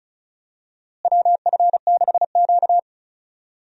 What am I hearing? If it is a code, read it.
Morse code, heard as a keyed tone, about 36 wpm, 700 Hz. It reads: WF6Q
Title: USA Call Signs - 4 Characters - Repeated 3x - 35wpm